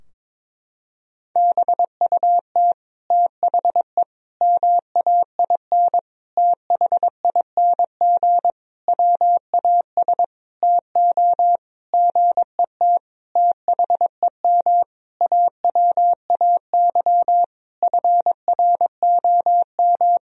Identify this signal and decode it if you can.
Morse code, heard as a keyed tone, about 22 wpm, 700 Hz. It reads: BUT THE MAIN THING WAS TO GET THEM AWAY FROM
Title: Sentences from Top 1000 Words - Review Part 2 - 22wpm